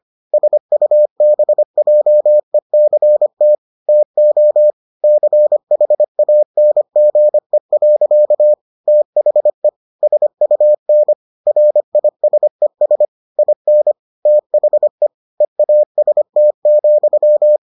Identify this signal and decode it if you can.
Morse code, heard as a keyed tone, about 25 wpm, 600 Hz. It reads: SUBJECT TO CHANGE. THE SUN RISES IN THE EAST,